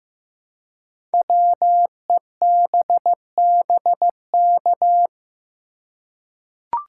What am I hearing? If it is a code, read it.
Morse code, heard as a keyed tone, about 15 wpm, 700 Hz. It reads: WEBBK
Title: Code Groups - 5 Letter:Numbers - 15wpm